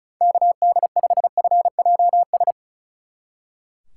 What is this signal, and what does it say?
Morse code, heard as a keyed tone, about 33 wpm, 700 Hz. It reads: KD5FJS
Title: Speed Racing - USA 2x3 Call Signs 35wpm